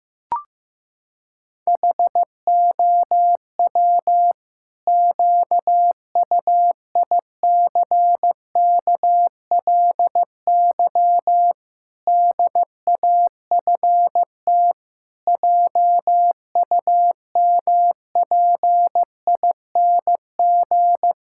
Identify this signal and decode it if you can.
Morse code, heard as a keyed tone, about 16 wpm, 700 Hz. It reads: HOW QUICKLY DAFT JUMPING
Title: Pangram Challenge v2 - 15wpm